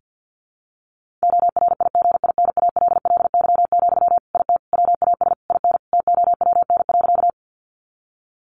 Morse code, heard as a keyed tone, about 50 wpm, 700 Hz. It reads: ORIGINALLY, IT WAS IN TOWN.
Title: Sentences from Top 1500 Words 50wpm